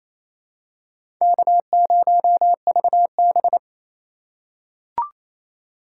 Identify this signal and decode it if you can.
Morse code, heard as a keyed tone, about 28 wpm, 700 Hz. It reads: K0VB
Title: USA Call Signs - 4x Character Spacing - 28wpm